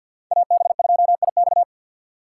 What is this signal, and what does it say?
Morse code, heard as a keyed tone, about 63 wpm, 700 Hz. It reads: AB2IX